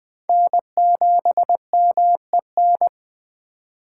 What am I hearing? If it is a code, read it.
Morse code, heard as a keyed tone, about 21 wpm, 700 Hz. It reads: N7MEN
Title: USA Call Signs - 5 Characters - 3x Character Spacing - 20wpm